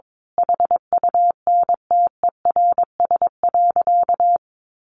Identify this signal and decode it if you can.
Morse code, heard as a keyed tone, about 22 wpm, 700 Hz. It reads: HUNTERS.
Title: A Sound Of Thunder 22wpm